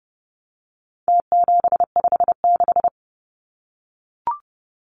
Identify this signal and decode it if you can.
Morse code, heard as a keyed tone, about 30 wpm, 700 Hz. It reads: T756
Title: Numbers - Contest Style - 4-Digit 30wpm